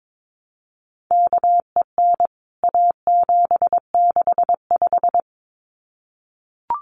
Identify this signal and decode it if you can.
Morse code, heard as a keyed tone, about 22 wpm, 700 Hz. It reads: KEN A765
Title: CWT Exchange 22wpm